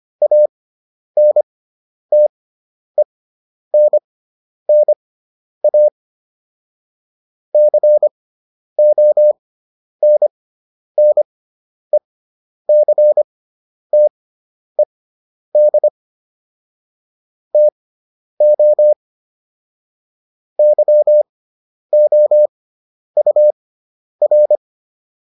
Morse code, heard as a keyed tone, about 25 wpm, spaced as for 10 wpm, 600 Hz. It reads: ANTENNA CONNECTED TO YOUR